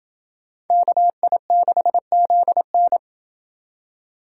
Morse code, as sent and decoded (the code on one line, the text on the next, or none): -.- .. -.... --.. -.
KI6ZN